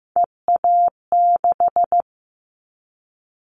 Morse code, heard as a keyed tone, about 15 wpm, 700 Hz. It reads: EA6
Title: Callsign Prefixes for Top 50 DXCC - 15wpm